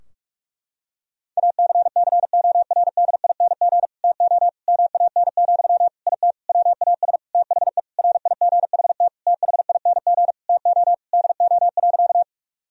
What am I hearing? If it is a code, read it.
Morse code, heard as a keyed tone, about 46 wpm, 700 Hz. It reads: ACCORDING TO MAN, IT WAS THE RIGHT THING TO DO.
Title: Sentences from Top 1000 Words - Review Part 4 - 45wpm